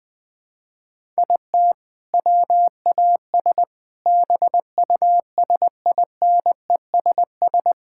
Morse code, heard as a keyed tone, about 20 wpm, 700 Hz. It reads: IT WAS BUSINESS
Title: Sentences from Top 1000 Words - Review Part 5 - 20wpm